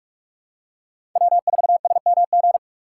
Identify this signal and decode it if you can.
Morse code, heard as a keyed tone, about 46 wpm, 700 Hz. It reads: W4SMG
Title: USA Call Signs - 5 Characters - 6x Character Spacing - 45wpm